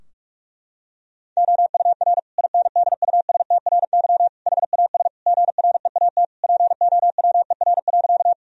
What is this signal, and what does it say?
Morse code, heard as a keyed tone, about 46 wpm, 700 Hz. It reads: OUR INDUSTRY HAS GREAT POWER.
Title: Sentences from Top 1000 Words - Review Part 4 - 45wpm